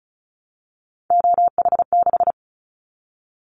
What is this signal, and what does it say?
Morse code, heard as a keyed tone, about 34 wpm, 700 Hz. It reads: OH6
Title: Callsign Prefixes for Top 25 DXCC - Repeated 3x - 35wpm